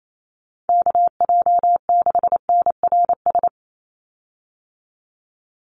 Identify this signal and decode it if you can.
Morse code, heard as a keyed tone, about 28 wpm, 700 Hz. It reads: KJ6NRS
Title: USA Call Signs 28wpm